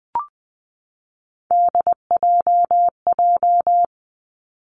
Morse code, heard as a keyed tone, about 20 wpm, 700 Hz. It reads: DJJ
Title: Three Challenging Letter-Number Combinations 20wpm